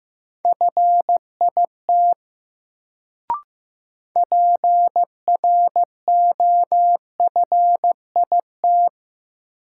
Morse code, heard as a keyed tone, about 15 wpm, 700 Hz. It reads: FIT PROFIT
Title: Sentences from Top 800 Words - New Word Drill - 15wpm